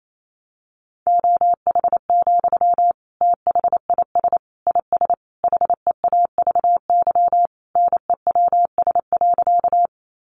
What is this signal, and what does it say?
Morse code, heard as a keyed tone, about 28 wpm, 700 Hz. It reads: OH, THIS IS HEAVY NEWS.
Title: Sentences from Top 1100 Words 28wpm